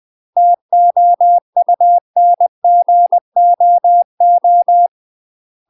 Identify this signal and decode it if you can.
Morse code, heard as a keyed tone, about 20 wpm, 700 Hz. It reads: TOUNGOO
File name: UK MASTER QSOs TRAINING / 20WPM / UK MASTER QSOs 0012